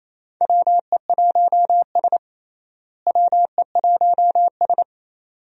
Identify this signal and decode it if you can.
Morse code, heard as a keyed tone, about 28 wpm, 700 Hz. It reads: WE1S WE1S
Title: USA Call Signs - 4 Characters - Repeated 2x - 28wpm